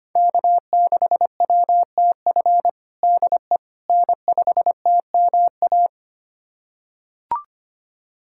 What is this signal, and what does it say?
Morse code, heard as a keyed tone, about 25 wpm, 700 Hz. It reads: K6WTF DE N5TMA
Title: USA Callsign DE Callsign - 5 Characters - 25wpm